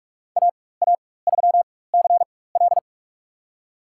Morse code, heard as a keyed tone, about 46 wpm, spaced as for 21 wpm, 700 Hz. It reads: AA3CL
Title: USA Call Signs - 4x Character Spacing - 45wpm